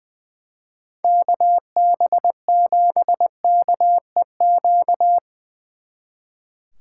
Morse code, heard as a keyed tone, about 23 wpm, 700 Hz. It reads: KB7KEQ
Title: Speed Racing - USA 2x3 Call Signs 20wpm